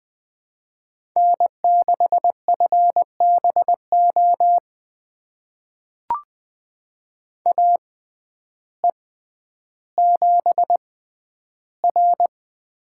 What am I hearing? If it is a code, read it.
Morse code, heard as a keyed tone, about 20 wpm, spaced as for 11 wpm, 700 Hz. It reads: N6FBO AE7R
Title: USA Call Signs - 6x Character Spacing - 20wpm